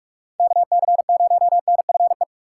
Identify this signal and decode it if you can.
Morse code, heard as a keyed tone, about 41 wpm, 700 Hz. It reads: KC0NFE
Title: USA Call Signs - 2x Character Spacing - 45wpm